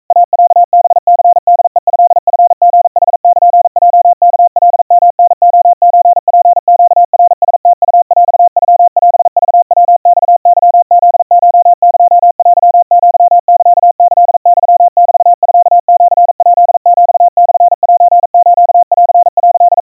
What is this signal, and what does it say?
Morse code (Japanese Wabun code), heard as a keyed tone, about 42 wpm, 700 Hz. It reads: イロハニホヘトチリヌルヲワカヨタレソツネナラムウヰノオクヤマケフコエテアサキユメミシヱヒモセスン」
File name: kawa / Morse sample / iroha 140 cpm 700 Hz Crispy